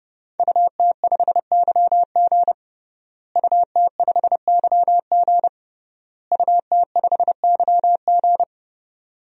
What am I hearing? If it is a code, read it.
Morse code, heard as a keyed tone, about 30 wpm, 700 Hz. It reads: UT5YG UT5YG UT5YG
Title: DXCC-100 Call Signs - 5 Characters - Repeated 3x - 30wpm